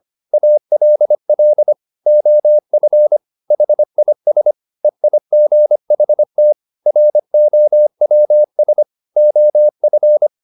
Morse code, heard as a keyed tone, about 25 wpm, 600 Hz. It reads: ALL OF HIS EIGHT ROWS OF